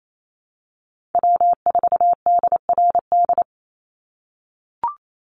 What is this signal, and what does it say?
Morse code, heard as a keyed tone, about 28 wpm, 700 Hz. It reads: W4DRD
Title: USA Call Signs 28wpm